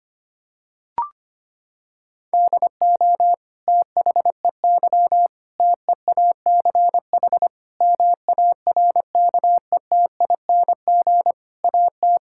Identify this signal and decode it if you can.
Morse code, heard as a keyed tone, about 25 wpm, 700 Hz. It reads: DO THEY TEACH MARKETING AT
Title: Sentences from Top 1000 Words - Review Part 5 - 25wpm